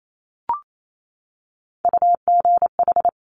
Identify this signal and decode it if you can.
Morse code, heard as a keyed tone, about 28 wpm, 700 Hz. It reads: UGH
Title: Top Three Letter Combinations in Words 28wpm